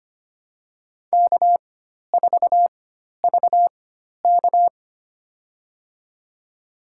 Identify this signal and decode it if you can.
Morse code, heard as a keyed tone, about 25 wpm, spaced as for 12 wpm, 700 Hz. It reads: K4VK
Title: USA Call Signs - 4 Characters - 4x Character Spacing - 25wpm